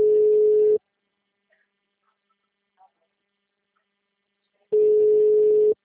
Ringback tone (Europe, 425 Hz).